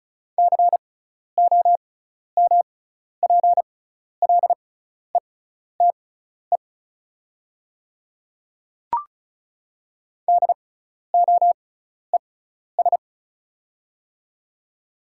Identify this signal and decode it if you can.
Morse code, heard as a keyed tone, about 36 wpm, spaced as for 12 wpm, 700 Hz. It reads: COMPLETE DOES
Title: 6x Character Spacing - Top 500 Words - 35wpm